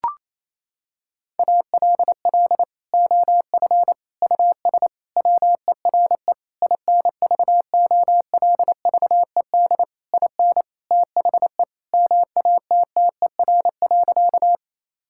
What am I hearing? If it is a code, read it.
Morse code, heard as a keyed tone, about 28 wpm, 700 Hz. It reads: ALL OF US WERE INVOLVED IN THE MATTER.
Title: Sentences from Top 1000 Words - Review Part 5 - 28wpm